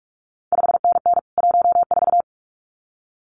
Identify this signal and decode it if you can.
Morse code, heard as a keyed tone, about 46 wpm, 700 Hz. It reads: 5NN 14
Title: CQ WW DX Contest Exchange - 45wpm